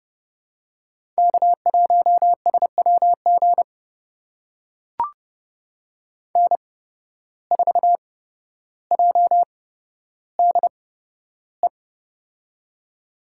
Morse code, heard as a keyed tone, about 30 wpm, spaced as for 13 wpm, 700 Hz. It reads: K1SWG N4JDE
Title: USA Call Signs - 5 Characters - 8x Character Spacing - 30wpm